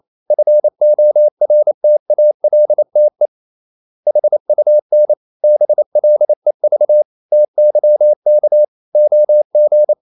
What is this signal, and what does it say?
Morse code, heard as a keyed tone, about 28 wpm, 600 Hz. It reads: FORTALTE HUN BLEV TYK OG
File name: Ferie QRQ 28wpm